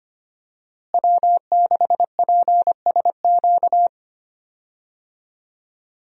Morse code, heard as a keyed tone, about 25 wpm, 700 Hz. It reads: W6PSQ
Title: USA Call Signs - 5 Characters - Repeated 3x - 25wpm